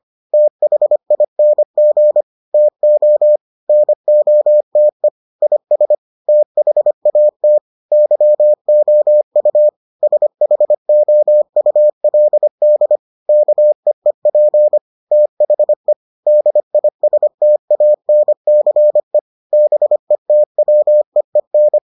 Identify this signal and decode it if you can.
Morse code, heard as a keyed tone, about 25 wpm, 600 Hz. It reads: THING TO NOTE IS THAT YOU SHOULD KEEP THE DISTANCE BETWEEN